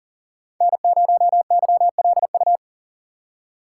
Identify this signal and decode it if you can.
Morse code, heard as a keyed tone, about 39 wpm, 700 Hz. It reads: N0YLU
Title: USA Call Signs - 5 Characters - 6x Character Spacing - 40wpm